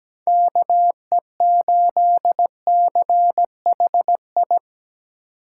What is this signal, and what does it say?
Morse code, heard as a keyed tone, about 17 wpm, 700 Hz. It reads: KE8CHI
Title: USA Call Signs - 8x Character Spacing - 17wpm